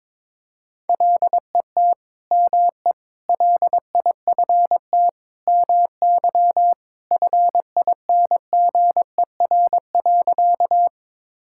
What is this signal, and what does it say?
Morse code, heard as a keyed tone, about 22 wpm, 700 Hz. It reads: LET ME LIFT MY FINGER.
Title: Sentences from Top 1600 Words 22wpm